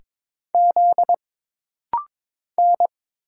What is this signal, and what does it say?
Morse code, heard as a keyed tone, about 22 wpm, 700 Hz. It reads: Z N
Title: Single Letter-Number Alt - Rapid-Fire 22wpm